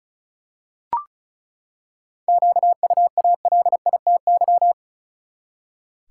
Morse code, heard as a keyed tone, about 34 wpm, 700 Hz. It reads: QUALITY
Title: Seven Letter Words - 35wpm